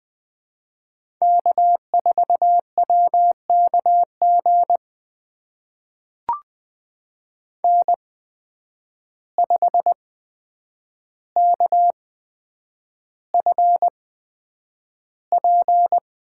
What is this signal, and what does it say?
Morse code, heard as a keyed tone, about 20 wpm, spaced as for 9 wpm, 700 Hz. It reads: K4WKG N5KFP
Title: USA Call Signs - 5 Characters - 8x Character Spacing - 20wpm